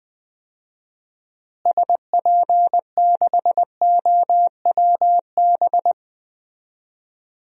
Morse code, heard as a keyed tone, about 20 wpm, 700 Hz. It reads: SP6OWB
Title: DXCC-100 Call Signs - 6 Characters - Repeated 2x - 20wpm